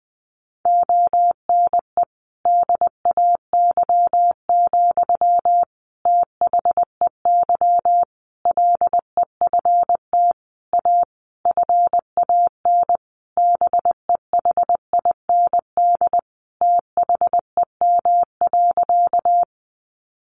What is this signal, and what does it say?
Morse code, heard as a keyed tone, about 19 wpm, 700 Hz. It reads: ONE DAY, THEY LEFT A FAN BEHIND THEM.
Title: Sentences from Top 1300 Words 20wpm